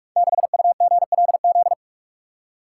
Morse code, heard as a keyed tone, about 44 wpm, 690 Hz. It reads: 6UGLZ
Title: Code Groups - 5 Letter:Numbers - 45wpm